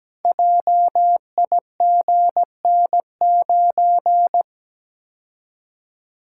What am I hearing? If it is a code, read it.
Morse code, heard as a keyed tone, about 17 wpm, 700 Hz. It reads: JIGN9
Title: Code Groups - 5 Letter:Numbers - 17wpm